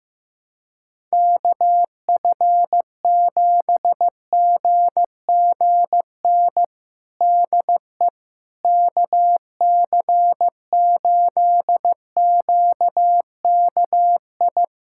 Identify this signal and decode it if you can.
Morse code, heard as a keyed tone, about 15 wpm, 700 Hz. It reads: KF7GGN DE KC8QKI
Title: USA Callsign DE Callsign 15wpm